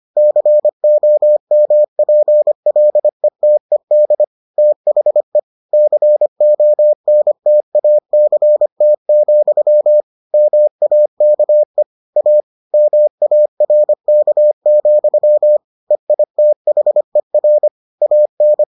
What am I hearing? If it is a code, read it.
Morse code, heard as a keyed tone, about 25 wpm, 600 Hz. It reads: COMPLETED THE CONTACT, MAKE A MARK, EITHER AN